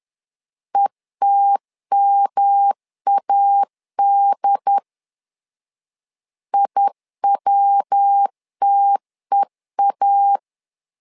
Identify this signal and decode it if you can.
Morse code, heard as a keyed tone, about 10 wpm, 800 Hz. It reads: ETMAD IWTEA
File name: LEZIONI / 3 MWRO / 3 MW 10wpm 1959